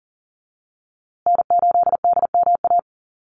Morse code, heard as a keyed tone, about 40 wpm, 700 Hz. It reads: N8DMA
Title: USA Call Signs Repeated 3x 40wpm